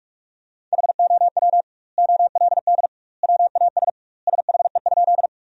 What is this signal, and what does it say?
Morse code, heard as a keyed tone, about 41 wpm, 700 Hz. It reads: HOW OLD WAS SHE?
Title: 2x Word-Spacing - Sentences from Top 200 Words - 45wpm